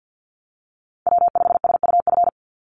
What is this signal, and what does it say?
Morse code, heard as a keyed tone, about 49 wpm, 700 Hz. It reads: W5SUF
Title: USA Call Signs 50wpm